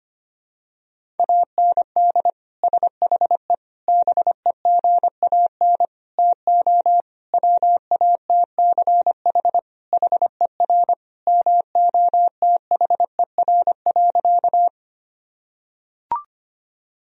Morse code, heard as a keyed tone, about 25 wpm, 700 Hz. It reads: AND SHE BEGAN TO WATCH HER MOTHER.
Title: Sentences from Top 1000 Words - Review Part 1 - 25wpm